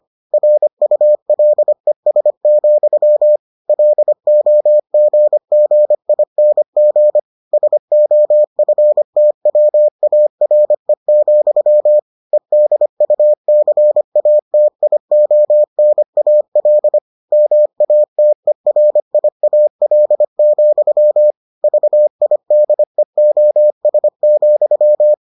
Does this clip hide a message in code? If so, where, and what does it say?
Yes: Morse code from 0.3 to 25.1 s, heard as a keyed tone, about 25 wpm, 600 Hz. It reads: RULES, LOGGING SOFTWARE, EDUCATIONAL MATERIAL, VIDEOS,